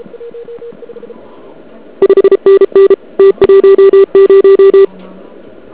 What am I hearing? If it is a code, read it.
Morse code, heard as a keyed tone, about 33 wpm, 390 Hz. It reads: 5NN T10
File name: ARCHIV / KX3 / KV PA